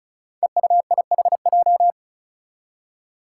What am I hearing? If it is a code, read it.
Morse code, heard as a keyed tone, about 36 wpm, 700 Hz. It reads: EUIHJ
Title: Five Challenging Letter-Number Combinations 35wpm